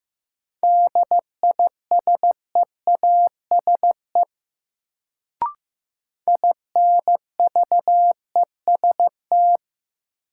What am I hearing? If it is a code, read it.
Morse code, heard as a keyed tone, about 15 wpm, 700 Hz. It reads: DISEASE INVEST
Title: Sentences from Top 1100 Words - New Word Drill Alt - 15wpm